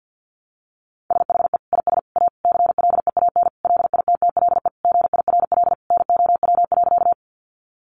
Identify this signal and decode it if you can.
Morse code, heard as a keyed tone, about 50 wpm, 700 Hz. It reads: SHE IS A CLEAN LITTLE GIRL NOW.
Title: Sentences from Top 1100 Words 50wpm